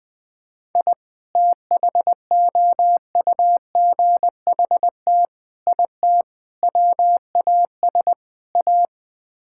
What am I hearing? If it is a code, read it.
Morse code, heard as a keyed tone, about 20 wpm, 700 Hz. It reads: I THOUGHT IT WAS A
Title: Build A Phrase - 5-Word Phrases - 20wpm